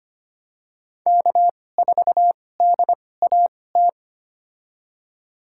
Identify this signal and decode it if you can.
Morse code, heard as a keyed tone, about 25 wpm, spaced as for 18 wpm, 700 Hz. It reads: K4DAT
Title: USA Call Signs - 5 Characters - 2x Character Spacing - 25wpm